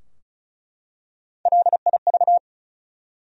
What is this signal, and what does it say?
Morse code, heard as a keyed tone, about 36 wpm, 700 Hz. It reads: LIV
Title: USA Common Names - 35wpm